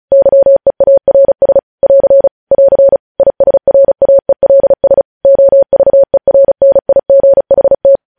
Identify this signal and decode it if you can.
Morse code, heard as a keyed tone, about 36 wpm, 550 Hz. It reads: YEARS <AR> <AR> ISRAELS OVERNIGHT
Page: news headlines in morse code at 35 wpm